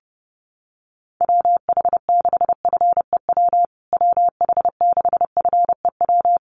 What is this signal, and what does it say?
Morse code, heard as a keyed tone, about 30 wpm, 700 Hz. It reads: WH6FEW WH6FEW
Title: USA Call Signs Repeated 3x 30wpm